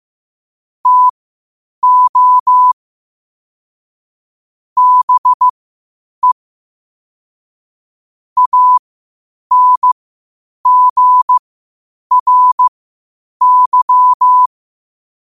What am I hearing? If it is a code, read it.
Morse code, heard as a keyed tone, about 15 wpm, spaced as for 8 wpm, 1000 Hz. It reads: TO BE ANGRY
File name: qotdfiles / 8wpm